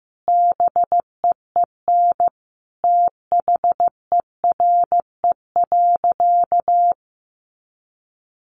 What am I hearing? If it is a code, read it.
Morse code, heard as a keyed tone, about 15 wpm, 700 Hz. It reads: BEEN THERE.
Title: Sentences from Top 300 Words 15wpm